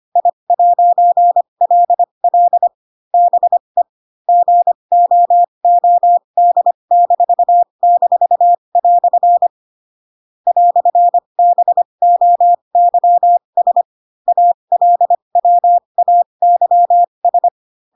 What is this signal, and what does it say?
Morse code, heard as a keyed tone, about 25 wpm, spaced as for 22 wpm, 700 Hz. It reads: I'LL BE GOOD--" "BOYS ALWAYS